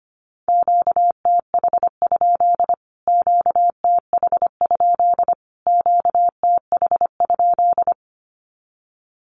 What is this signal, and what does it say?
Morse code, heard as a keyed tone, about 25 wpm, 700 Hz. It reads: QTH? QTH? QTH?
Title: Q-Codes Repeated 3x 25wpm